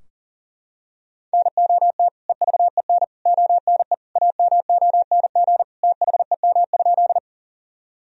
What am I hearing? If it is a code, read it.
Morse code, heard as a keyed tone, about 40 wpm, 700 Hz. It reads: NOT EVEN ONE AMONG THEM?
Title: Sentences from Top 1000 Words - Review Part 3 - 40wpm